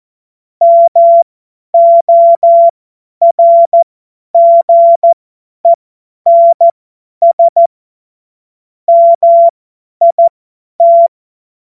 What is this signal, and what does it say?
Morse code, heard as a keyed tone, about 14 wpm, spaced as for 10 wpm, 680 Hz. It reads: MORGENS MIT
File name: Morsekurs / Witze / tz 08. November 2022 50 F70